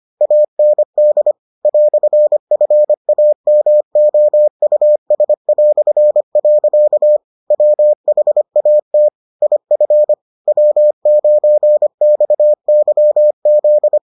Morse code, heard as a keyed tone, about 25 wpm, 600 Hz. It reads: AND "FAMOUS". WHAT IF W9XYZ